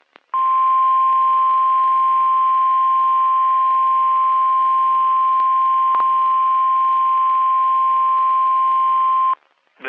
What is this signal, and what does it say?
Emergency Alert System (SAME) tones (the 1050 Hz attention tone). It is heard from 0.3 to 9.4 s.
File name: eas messages / EAS WXM31 Elmira